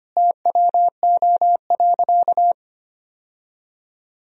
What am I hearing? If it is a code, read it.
Morse code, heard as a keyed tone, about 25 wpm, 700 Hz. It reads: TWO.
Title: Sentences from Top 1000 Words - Review Part 1 - 25wpm